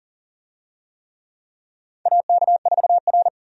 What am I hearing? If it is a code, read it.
Morse code, heard as a keyed tone, about 40 wpm, 700 Hz. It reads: AK4R
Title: USA Call Signs - 4 Characters - 40wpm